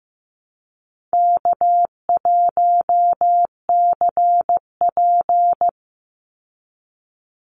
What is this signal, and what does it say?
Morse code, heard as a keyed tone, about 15 wpm, 700 Hz. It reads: K1CP
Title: Call Signs Repeated 2x - 15wpm